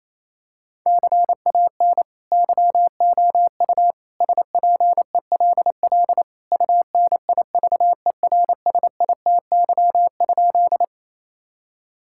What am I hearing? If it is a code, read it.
Morse code, heard as a keyed tone, about 28 wpm, 700 Hz. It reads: CAN YOU SPELL UNIVERSITY?
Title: Sentences from Top 1000 Words - Review Part 5 - 28wpm